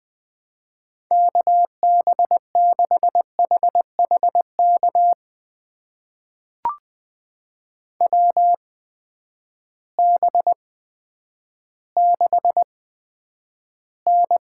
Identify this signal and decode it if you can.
Morse code, heard as a keyed tone, about 20 wpm, spaced as for 10 wpm, 700 Hz. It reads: KB6HHK WB6N
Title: USA Call Signs - 6 Characters - 8x Character Spacing - 20wpm